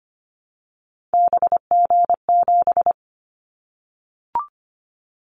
Morse code, heard as a keyed tone, about 25 wpm, 700 Hz. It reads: BG7
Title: Callsign Prefixes for Top 100 DXCC - 25wpm